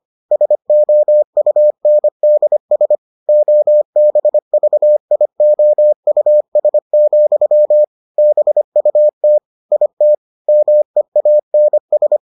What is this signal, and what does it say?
Morse code, heard as a keyed tone, about 25 wpm, 600 Hz. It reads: SOUNDS OBVIOUS, BUT IT MEANS